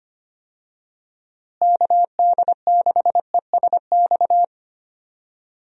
Morse code, heard as a keyed tone, about 25 wpm, 700 Hz. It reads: KD6ESX
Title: USA Call Signs - 6 Characters - Repeated 2x - 25wpm